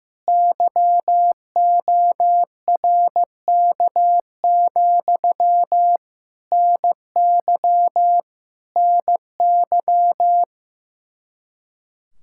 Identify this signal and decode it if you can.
Morse code, heard as a keyed tone, about 15 wpm, 700 Hz. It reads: YORK, NY NY
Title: Practice QSO - Exchange 1 - Beginning - 15wpm